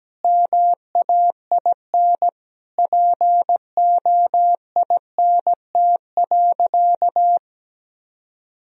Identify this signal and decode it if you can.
Morse code, heard as a keyed tone, about 17 wpm, 700 Hz. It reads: MAIN POINT.
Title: Sentences from Top 1000 Words - Review Part 2 - 17wpm